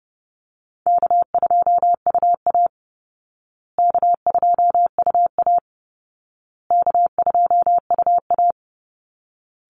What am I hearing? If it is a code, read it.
Morse code, heard as a keyed tone, about 30 wpm, 700 Hz. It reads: K2UA K2UA K2UA
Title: Call Signs Repeated 2x - 30wpm